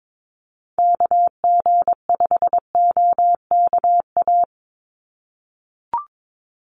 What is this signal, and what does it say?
Morse code, heard as a keyed tone, about 22 wpm, 700 Hz. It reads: KG5OKA
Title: USA Call Signs 22wpm